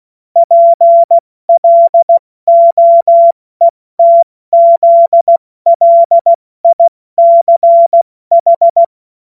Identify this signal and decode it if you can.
Morse code, heard as a keyed tone, about 16 wpm, spaced as for 14 wpm, 680 Hz. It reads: PLOETZLICH
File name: Morsekurs / Witze / Witz 17. August 2021, 70 F80